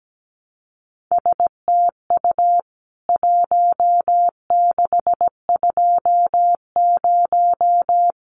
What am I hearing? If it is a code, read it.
Morse code, heard as a keyed tone, about 17 wpm, 700 Hz. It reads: STU 1620
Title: CWT Exchange 17wpm